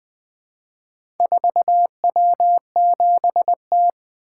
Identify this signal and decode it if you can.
Morse code, heard as a keyed tone, about 20 wpm, 700 Hz. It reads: 4W7T
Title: Four Letter-Number Combinations - 20wpm